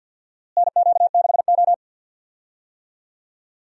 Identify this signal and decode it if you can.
Morse code, heard as a keyed tone, about 51 wpm, 700 Hz. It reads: NQ6O
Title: USA Call Signs - 4 Characters - Repeated 3x - 50wpm